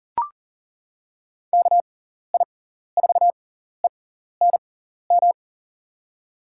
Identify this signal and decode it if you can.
Morse code, heard as a keyed tone, about 38 wpm, spaced as for 14 wpm, 700 Hz. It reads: KI4ENM
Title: USA Call Signs - 6 Characters - 6x Character Spacing - 40wpm